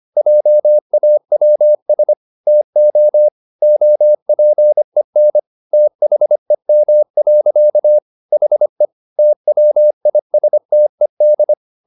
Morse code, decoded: JAWS TO OPEN THEM. HE TWISTED

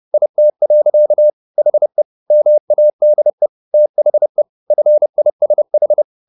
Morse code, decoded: IT. HE MADE THE FISH

